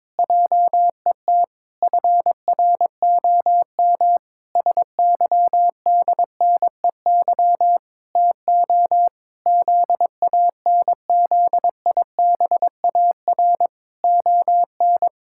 Morse code, heard as a keyed tone, about 22 wpm, 700 Hz. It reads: JET FROM SYDNEY TO ZANZIBAR ON
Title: Pangram Challenge v2 - 22wpm